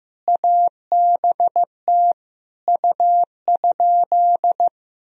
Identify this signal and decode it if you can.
Morse code, heard as a keyed tone, about 15 wpm, 700 Hz. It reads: ABT U?
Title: Sentences from Top 100 Words and Abbreviations 15wpm